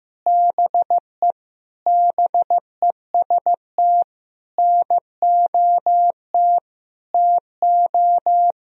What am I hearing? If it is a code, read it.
Morse code, heard as a keyed tone, about 14 wpm, 700 Hz. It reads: BE BEST NOT TO
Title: Sentences from Top 1000 Words - Review Part 1 - 15wpm